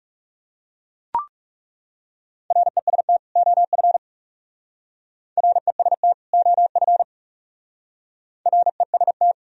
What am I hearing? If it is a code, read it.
Morse code, heard as a keyed tone, about 39 wpm, 700 Hz. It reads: REST OF REST OF REST